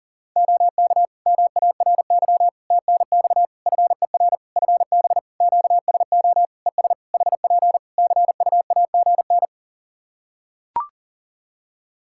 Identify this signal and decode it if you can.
Morse code, heard as a keyed tone, about 40 wpm, 700 Hz. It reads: OK MARY TNX FER FB QSO ES HP CUAGN
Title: Practice QSO - Exchange 4 - Ending - 40wpm